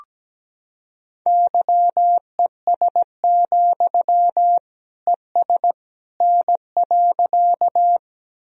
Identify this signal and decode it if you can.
Morse code, heard as a keyed tone, about 17 wpm, 700 Hz. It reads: YES, ES N.
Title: Sentences from Top 500 Words and Abbreviations 17wpm